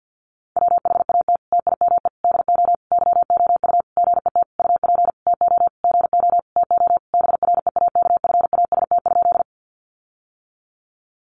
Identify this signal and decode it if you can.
Morse code, heard as a keyed tone, about 50 wpm, 700 Hz. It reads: WHAT TIME DO YOU GET UP TO GO TO BREAKFAST?
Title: Sentences from Top 1500 Words 50wpm